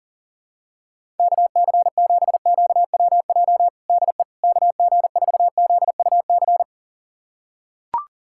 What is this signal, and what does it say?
Morse code, heard as a keyed tone, about 40 wpm, 700 Hz. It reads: KC7QWJ DE KG4ZUC
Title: USA Callsign DE Callsign - 6 Characters - 40wpm